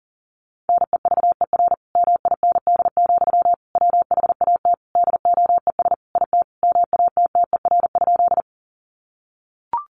Morse code, heard as a keyed tone, about 40 wpm, 700 Hz. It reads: NEVER MIND, WHAT DOES IT MATTER?
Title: Sentences from Top 700 Words 40wpm